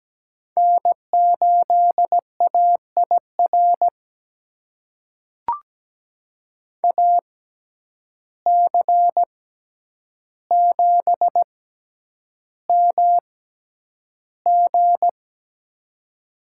Morse code, heard as a keyed tone, about 17 wpm, spaced as for 9 wpm, 700 Hz. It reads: N8AIR AC7MG
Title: USA Call Signs - 5 Characters - 6x Character Spacing - 17wpm